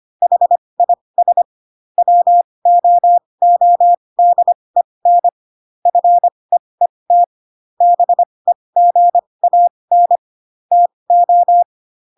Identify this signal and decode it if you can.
Morse code, heard as a keyed tone, about 25 wpm, spaced as for 20 wpm, 700 Hz. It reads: HIS WOODEN FEET BEGAN TO
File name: Pinocchio / Pinocchio ch6 15 25 ik1hge